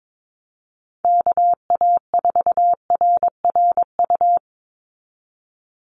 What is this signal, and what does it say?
Morse code, heard as a keyed tone, about 22 wpm, 700 Hz. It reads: KA4RRU
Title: Call Signs Repeated 3x - 22wpm